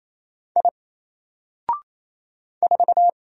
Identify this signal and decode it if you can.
Morse code, heard as a keyed tone, about 28 wpm, 700 Hz. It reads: I 4
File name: Single-Letter-Number-28wpm